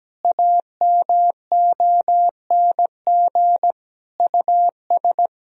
Morse code, heard as a keyed tone, about 17 wpm, 700 Hz. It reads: AMONG US
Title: Sentences from Top 1000 Words - Review Part 3 - 17wpm